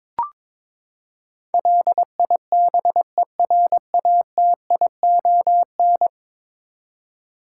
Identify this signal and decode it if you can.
Morse code, heard as a keyed tone, about 22 wpm, 700 Hz. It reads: LIBERATION
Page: morse-code-ninja-practice-22wpm